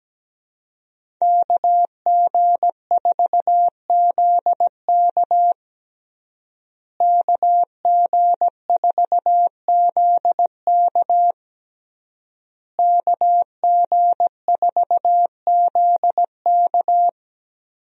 Morse code, heard as a keyed tone, about 17 wpm, 700 Hz. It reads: KG4ZK KG4ZK KG4ZK
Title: USA Call Signs - 5 Characters - Repeated 3x - 17wpm